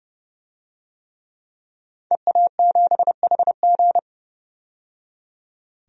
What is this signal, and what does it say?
Morse code, heard as a keyed tone, about 30 wpm, 700 Hz. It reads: EA7HG
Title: DXCC-100 Call Signs - 5 Characters - 30wpm